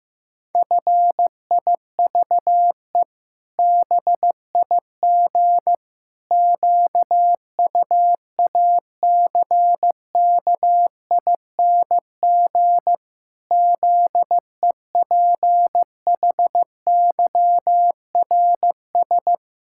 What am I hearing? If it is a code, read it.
Morse code, heard as a keyed tone, about 16 wpm, 700 Hz. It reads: FIVE BIG QUACKING ZEPHYRS
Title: Pangram Challenge v2 - 15wpm